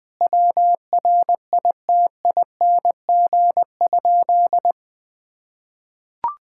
Morse code, heard as a keyed tone, about 19 wpm, 700 Hz. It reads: WRITING?
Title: Sentences from Top 1000 Words - Review Part 6 - 20wpm